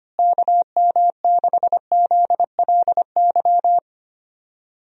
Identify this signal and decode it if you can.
Morse code, heard as a keyed tone, about 25 wpm, 700 Hz. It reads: KM6ZLY
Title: USA Call Signs - 6 Characters - Repeated 2x - 25wpm